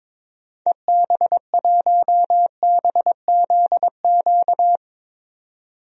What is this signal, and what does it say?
Morse code, heard as a keyed tone, about 22 wpm, 700 Hz. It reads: EB1BZQ
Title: DXCC-100 Call Signs - 6 Characters - 22wpm